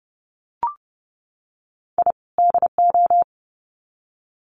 Morse code, heard as a keyed tone, about 30 wpm, 700 Hz. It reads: I DO
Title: Sets of 2 Words 30wpm